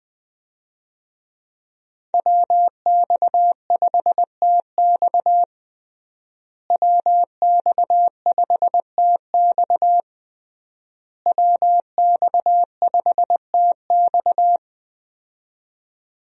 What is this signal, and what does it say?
Morse code, heard as a keyed tone, about 20 wpm, 700 Hz. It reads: WX5TX WX5TX WX5TX
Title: USA Call Signs - 5 Characters - Repeated 3x - 20wpm